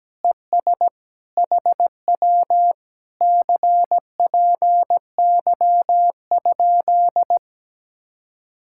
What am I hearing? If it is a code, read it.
Morse code, heard as a keyed tone, about 17 wpm, 700 Hz. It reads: ES HW CPY?
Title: Speed-Racing Common QSO Phrases - 17wpm